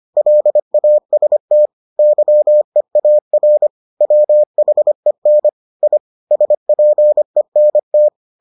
Morse code, heard as a keyed tone, about 25 wpm, 600 Hz. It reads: LAST YEAR WHEN I SPENT